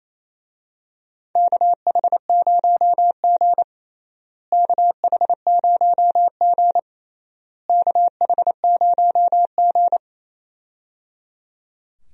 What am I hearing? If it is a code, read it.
Morse code, heard as a keyed tone, about 28 wpm, 700 Hz. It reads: KH0G KH0G KH0G
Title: USA Call Signs - 4 Characters - Repeated 3x - 28wpm